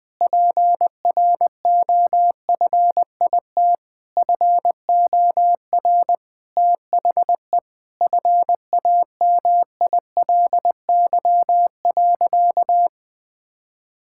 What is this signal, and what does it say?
Morse code, heard as a keyed tone, about 20 wpm, 700 Hz. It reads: PROFIT FOR THE FAMILY.
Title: Sentences from Top 1000 Words - Review Part 5 - 20wpm